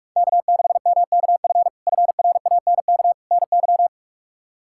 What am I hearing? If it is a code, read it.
Morse code, heard as a keyed tone, about 46 wpm, 700 Hz. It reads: K6MKF FRANK NY
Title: K1USN SST QSO Practice - 45wpm